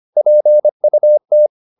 Morse code, heard as a keyed tone, about 25 wpm, 600 Hz. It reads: PUT